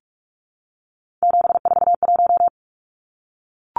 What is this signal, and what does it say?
Morse code, heard as a keyed tone, about 44 wpm, 700 Hz. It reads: 741
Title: Numbers - 3-Digit 45wpm